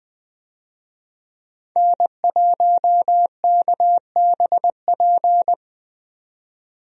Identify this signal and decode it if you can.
Morse code, heard as a keyed tone, about 20 wpm, 700 Hz. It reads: N1KBP
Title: USA Call Signs - 5 Characters - 20wpm